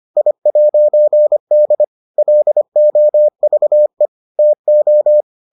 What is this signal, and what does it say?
Morse code, heard as a keyed tone, about 25 wpm, 600 Hz. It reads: I'D LOVE TO